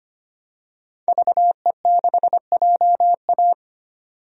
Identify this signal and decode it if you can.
Morse code, heard as a keyed tone, about 25 wpm, 700 Hz. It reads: VE6JA
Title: DXCC-100 Call Signs - 5 Characters - 25wpm